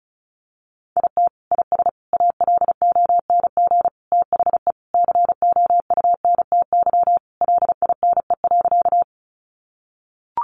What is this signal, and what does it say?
Morse code, heard as a keyed tone, about 34 wpm, 700 Hz. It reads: IT IS ALONG THE COUNTY LINE.
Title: Sentences from Top 1200 Words 35wpm